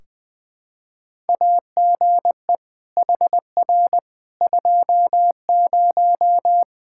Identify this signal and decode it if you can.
Morse code, heard as a keyed tone, about 21 wpm, 700 Hz. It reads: AGE HR 20
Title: Practice QSO - Exchange 3 - More Info - 20wpm